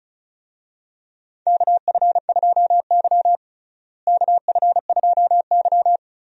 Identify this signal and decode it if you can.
Morse code, heard as a keyed tone, about 34 wpm, 700 Hz. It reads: KF2Y KF2Y
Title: USA Call Signs - 4 Characters - Repeated 2x - 35wpm